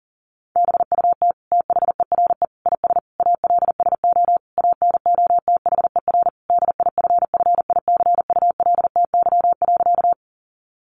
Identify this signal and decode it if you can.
Morse code, heard as a keyed tone, about 40 wpm, 700 Hz. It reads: BUT THERE IS ALSO ANOTHER DIFFICULTY.
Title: Sentences from Top 1500 Words 40wpm